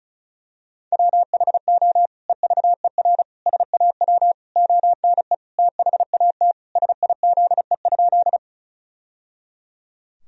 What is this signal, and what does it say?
Morse code, heard as a keyed tone, about 34 wpm, 700 Hz. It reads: WHO EVER SAW ONE THAT SIZE?
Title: Sentences from Top 1000 Words - Review Part 2 - 35wpm